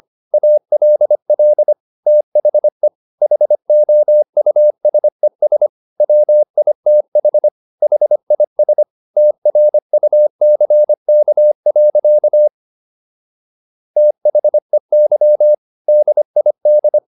Morse code, heard as a keyed tone, about 25 wpm, 600 Hz. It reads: ALL THE HOUSES WITH HIS TRUCK. THEY DID